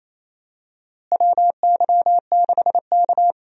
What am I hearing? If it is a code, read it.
Morse code, heard as a keyed tone, about 28 wpm, 700 Hz. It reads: WY6K
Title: USA Call Signs - 4 Characters - Repeated 3x - 28wpm